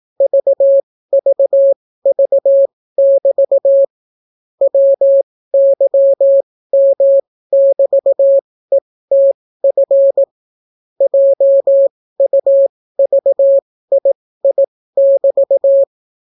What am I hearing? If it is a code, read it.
Morse code, heard as a keyed tone, about 18 wpm, spaced as for 15 wpm, 550 Hz. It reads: VVV<BT> WYM<BT>ETF JUVII<BT>